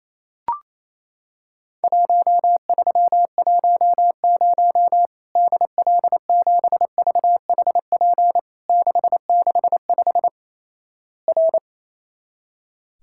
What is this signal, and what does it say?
Morse code, heard as a keyed tone, about 27 wpm, 700 Hz. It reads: 1310 DL7VHP 665 R
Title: Worked All Europe DX Contest - QTC Exchange - 28wpm